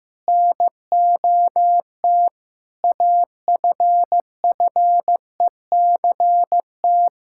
Morse code, heard as a keyed tone, about 15 wpm, 700 Hz. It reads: NOT AFFECT
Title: Sentences from Top 1000 Words - Review Part 6 - 15wpm